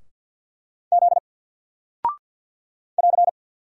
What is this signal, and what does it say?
Morse code, heard as a keyed tone, about 50 wpm, 700 Hz. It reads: Z <AR>
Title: Single Character (Letter:Number:Punctuation:Prosign) 50wpm